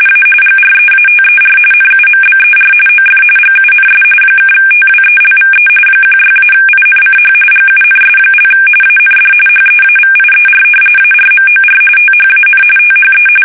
RTTY, about 45.45 baud. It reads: A GREATER NUMBER OF GUNS (TWELVE 16-INCH GUNS) AND HEAVIER ARMOR THAN THE
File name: fsk850 / battleships